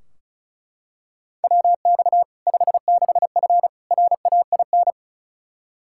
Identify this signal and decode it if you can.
Morse code, heard as a keyed tone, about 36 wpm, 700 Hz. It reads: WX 56F RAIN
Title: Practice QSO - Exchange 2 - Info - 35wpm